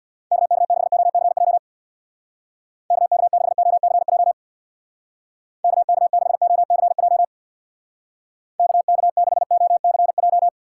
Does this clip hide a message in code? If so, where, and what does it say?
Yes: Morse code from 0.3 to 10.5 s, heard as a keyed tone, about 63 wpm, 700 Hz. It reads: KK6OCJ KK6OCJ KK6OCJ KK6OCJ